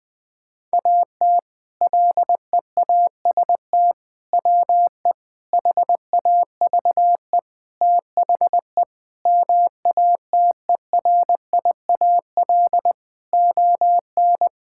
Morse code, heard as a keyed tone, about 20 wpm, 700 Hz. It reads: AT LEAST WE HAVE THE MATERIAL ON
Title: Sentences from Top 1000 Words - Review Part 5 - 20wpm